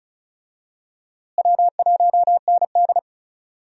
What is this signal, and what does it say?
Morse code, heard as a keyed tone, about 36 wpm, 700 Hz. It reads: W1ND
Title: USA Call Signs - 4 Characters - Repeated 2x - 35wpm